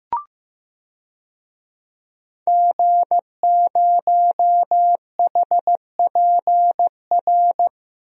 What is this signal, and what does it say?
Morse code, heard as a keyed tone, about 15 wpm, 700 Hz. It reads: G0HPR
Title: DXCC-100 Call Signs - 5 Characters - Repeated 3x - 15wpm